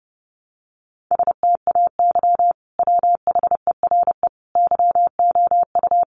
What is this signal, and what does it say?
Morse code, heard as a keyed tone, about 29 wpm, 700 Hz. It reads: STAY WHERE YOU
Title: Sentences from Top 500 Words 30wpm